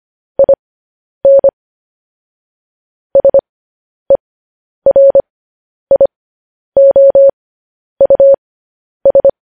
Morse code, heard as a keyed tone, about 25 wpm, spaced as for 10 wpm, 550 Hz. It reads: IN SERIOUS
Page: news headlines in morse code at 10 wpm